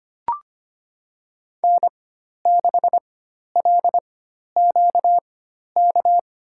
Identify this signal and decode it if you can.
Morse code, heard as a keyed tone, about 25 wpm, spaced as for 12 wpm, 700 Hz. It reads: N6LQK
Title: USA Call Signs - 5 Characters - 4x Character Spacing - 25wpm